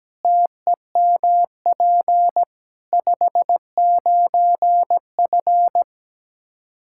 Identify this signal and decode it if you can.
Morse code, heard as a keyed tone, about 17 wpm, 700 Hz. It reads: TEMP 59F
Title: Practice QSO - Exchange 2 - Info - 17wpm